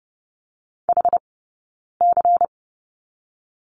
Morse code, heard as a keyed tone, about 30 wpm, 700 Hz. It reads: H C
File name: Lesson 20d - Number 3 - Two Characters 30wpm